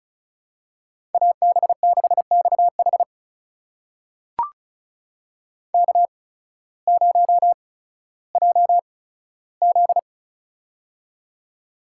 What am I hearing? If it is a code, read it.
Morse code, heard as a keyed tone, about 36 wpm, spaced as for 16 wpm, 700 Hz. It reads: AB6XH K0JZ
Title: USA Call Signs - 8x Character Spacing - 35wpm